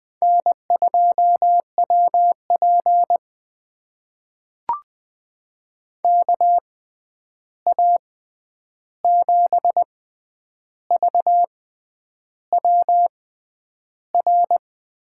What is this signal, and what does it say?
Morse code, heard as a keyed tone, about 20 wpm, spaced as for 9 wpm, 700 Hz. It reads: N2WP KA7VWR
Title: USA Call Signs - 6x Character Spacing - 20wpm